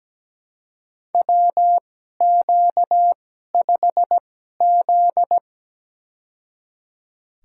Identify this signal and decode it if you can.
Morse code, heard as a keyed tone, about 17 wpm, spaced as for 12 wpm, 700 Hz. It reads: WQ5Z
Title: USA Call Signs - 4 Characters - 2x Character Spacing - 17wpm